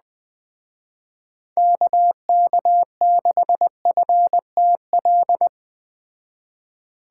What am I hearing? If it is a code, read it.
Morse code, heard as a keyed tone, about 20 wpm, 700 Hz. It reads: KK6FTL